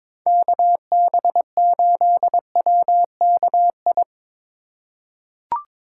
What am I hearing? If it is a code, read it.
Morse code, heard as a keyed tone, about 22 wpm, 700 Hz. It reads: KB8WKI
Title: USA Call Signs - 6 Characters - 8x Character Spacing - 22wpm